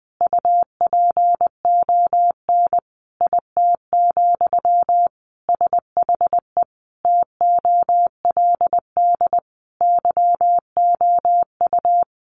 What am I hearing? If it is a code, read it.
Morse code, heard as a keyed tone, about 20 wpm, 700 Hz. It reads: UPON IT, SHE TOLD YOU
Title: Sentences from Top 1300 Words 20wpm